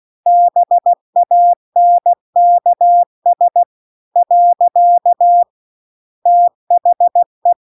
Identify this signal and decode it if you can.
Morse code, heard as a keyed tone, about 16 wpm, 700 Hz. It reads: BANKS . THE
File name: CW-The-Big-Twoheaded-River-2-16WPM